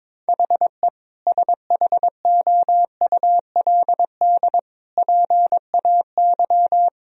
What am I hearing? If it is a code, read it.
Morse code, heard as a keyed tone, about 22 wpm, 700 Hz. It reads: HE SHOULD PAY
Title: Sentences from Top 1000 Words - Review Part 2 - 22wpm